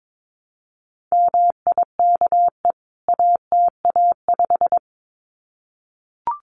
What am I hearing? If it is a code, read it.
Morse code, heard as a keyed tone, about 22 wpm, 700 Hz. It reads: MIKE ATA5
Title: CWT Exchange 22wpm